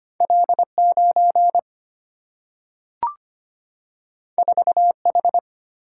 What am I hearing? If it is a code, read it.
Morse code, heard as a keyed tone, about 25 wpm, 700 Hz. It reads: L9 4H
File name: Two Letter-Number Combinations - 25wpm